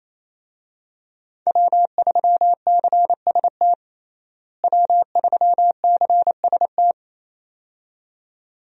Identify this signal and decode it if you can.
Morse code, heard as a keyed tone, about 28 wpm, 700 Hz. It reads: W3CST W3CST
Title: USA Call Signs - 5 Characters - Repeated 2x - 28wpm